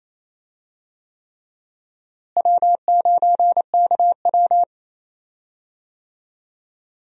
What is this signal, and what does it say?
Morse code, heard as a keyed tone, about 28 wpm, 700 Hz. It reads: W9KW
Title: USA Call Signs - 4 Characters - Repeated 3x - 28wpm